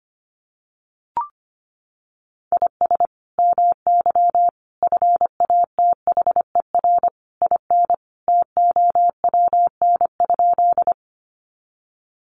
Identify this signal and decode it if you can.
Morse code, heard as a keyed tone, about 25 wpm, 700 Hz. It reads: IS MY FATHER IN TOWN?
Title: Sentences from Top 500 Words 25wpm